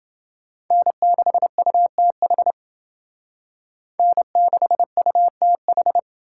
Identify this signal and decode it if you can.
Morse code, heard as a keyed tone, about 29 wpm, 700 Hz. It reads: N6UTH N6UTH